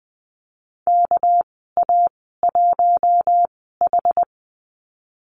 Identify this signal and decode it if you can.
Morse code, heard as a keyed tone, about 20 wpm, spaced as for 15 wpm, 700 Hz. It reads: KA1H
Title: 200 Lesson 32g - Letter k - Call Signs 20wpm